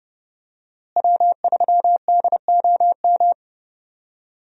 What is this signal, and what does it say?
Morse code, heard as a keyed tone, about 31 wpm, 700 Hz. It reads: W3DOM
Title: USA Call Signs - 5 Characters - Repeated 2x - 30wpm